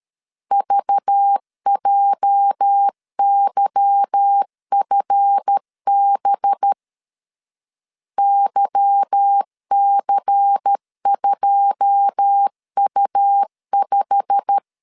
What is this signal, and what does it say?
Morse code, heard as a keyed tone, about 13 wpm, 800 Hz. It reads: VJYFB YC2U5
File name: LEZIONI / 9 2 12wpm 2301